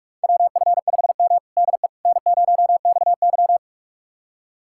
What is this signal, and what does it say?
Morse code, heard as a keyed tone, about 42 wpm, 700 Hz. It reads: WF5M DE N0XY
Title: USA Callsign DE Callsign - 4 Characters - 45wpm